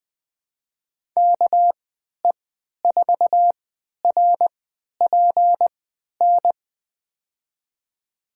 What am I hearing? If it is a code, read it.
Morse code, heard as a keyed tone, about 20 wpm, spaced as for 11 wpm, 700 Hz. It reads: KE4RPN
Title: USA Call Signs - 3x Character Spacing - 20wpm